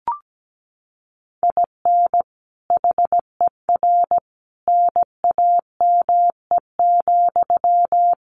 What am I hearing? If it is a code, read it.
Morse code, heard as a keyed tone, about 17 wpm, 700 Hz. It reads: IN HER NAME,
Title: Sentences from Top 500 Words 17wpm